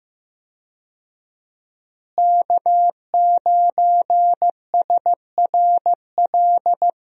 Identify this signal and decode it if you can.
Morse code, heard as a keyed tone, about 15 wpm, 700 Hz. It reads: K9SRL
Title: USA Call Signs - 5 Characters - Repeated 2x - 15wpm